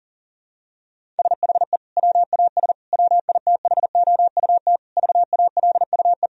Morse code, heard as a keyed tone, about 37 wpm, 700 Hz. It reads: SHE WAS WITHOUT VALUE
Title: Sentences from Top 1000 Words - Review Part 3 - 40wpm